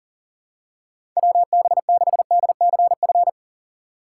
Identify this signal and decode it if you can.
Morse code, heard as a keyed tone, about 40 wpm, 700 Hz. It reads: WB6DCF
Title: USA Call Signs - 6 Characters - 4x Character Spacing - 40wpm